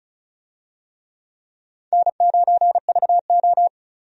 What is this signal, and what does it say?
Morse code, heard as a keyed tone, about 36 wpm, 700 Hz. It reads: N9VO
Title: USA Call Signs - 4 Characters - Repeated 2x - 35wpm